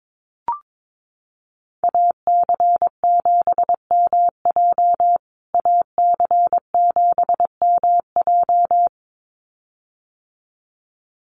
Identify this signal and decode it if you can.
Morse code, heard as a keyed tone, about 22 wpm, 700 Hz. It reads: AC7MJ AC7MJ
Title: USA Call Signs Repeated 2x 22wpm